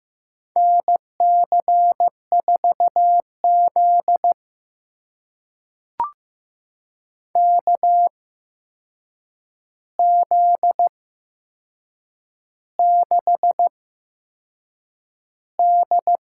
Morse code, heard as a keyed tone, about 15 wpm, spaced as for 6 wpm, 700 Hz. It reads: NC4Z KZ6D
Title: USA Call Signs - 4 Characters - 8x Character Spacing - 15wpm